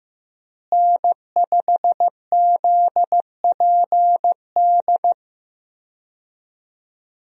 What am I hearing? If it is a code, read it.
Morse code, heard as a keyed tone, about 15 wpm, 700 Hz. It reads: N5ZPD
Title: USA Call Signs - 5 Characters - Repeated 2x - 15wpm